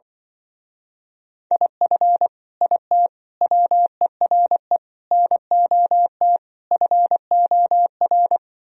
Morse code, heard as a keyed tone, about 24 wpm, 700 Hz. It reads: IF IT WERE NOT FOR
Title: Speed-Racing - Sets of 5 Words 20wpm